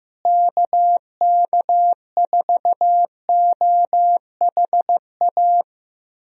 Morse code, heard as a keyed tone, about 15 wpm, 700 Hz. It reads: KK4OHA
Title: USA Call Signs - 6 Characters - Repeated 3x - 15wpm